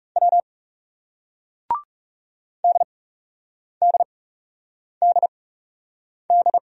Morse code, heard as a keyed tone, about 41 wpm, spaced as for 8 wpm, 700 Hz. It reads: W DDDD